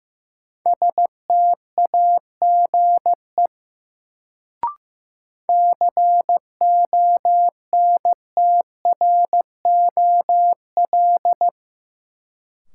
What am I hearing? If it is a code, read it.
Morse code, heard as a keyed tone, about 15 wpm, 700 Hz. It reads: STAGE CONTROL
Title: Sentences from Top 1200 Words - New Word Drill Alt - 15wpm